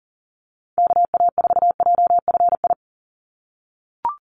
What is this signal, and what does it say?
Morse code, heard as a keyed tone, about 40 wpm, 700 Hz. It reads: KA4JFI
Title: USA Call Signs 40wpm